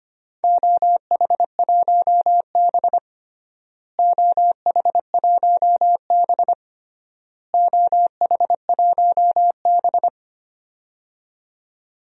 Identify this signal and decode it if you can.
Morse code, heard as a keyed tone, about 25 wpm, 700 Hz. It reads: OH1B OH1B OH1B
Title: DXCC-100 Call Signs - 4 Characters - Repeated 3x - 25wpm